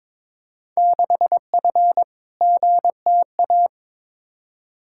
Morse code, heard as a keyed tone, about 22 wpm, 700 Hz. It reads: 6F GTA
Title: Field Day Exchanges Encore - 22wpm